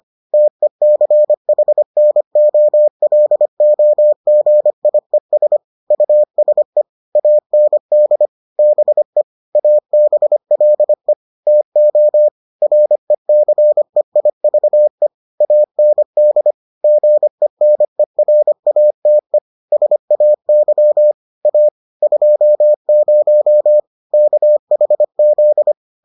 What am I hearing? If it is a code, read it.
Morse code, heard as a keyed tone, about 25 wpm, 600 Hz. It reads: TECHNOLOGIES USE AND BE ABLE TO RECEIVE AND GENERATE SAY A 20 KHZ